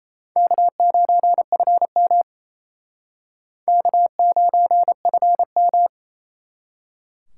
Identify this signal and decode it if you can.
Morse code, heard as a keyed tone, about 35 wpm, 700 Hz. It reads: K9FM K9FM